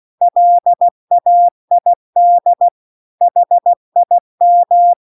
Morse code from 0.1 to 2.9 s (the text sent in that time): LAID